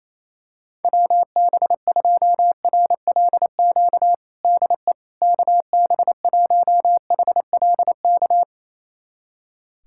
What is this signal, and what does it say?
Morse code, heard as a keyed tone, about 28 wpm, 700 Hz. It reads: WB2RLQ DE KB1HLK
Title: USA Callsign DE Callsign - 6 Characters - 28wpm